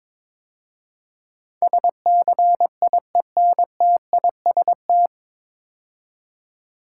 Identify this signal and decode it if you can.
Morse code, heard as a keyed tone, about 21 wpm, 700 Hz. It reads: SCIENTIST
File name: 9 Letter Words - First 3 Letters Given - 22wpm